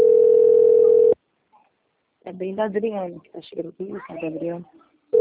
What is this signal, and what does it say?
Ringback tone (North America), 440 + 480 Hz, with speech.
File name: out- - - - - .85903